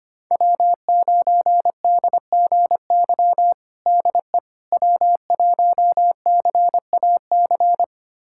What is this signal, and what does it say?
Morse code, heard as a keyed tone, about 25 wpm, 700 Hz. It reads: W9DGY DE W1CAC
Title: USA Callsign DE Callsign - 5 Characters - 25wpm